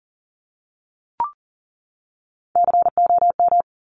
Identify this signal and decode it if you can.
Morse code, heard as a keyed tone, about 40 wpm, 700 Hz. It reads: COM
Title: Top Three Letter Combinations in Words 40wpm